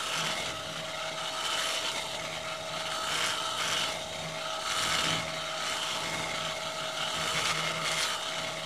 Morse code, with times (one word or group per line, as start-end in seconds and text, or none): none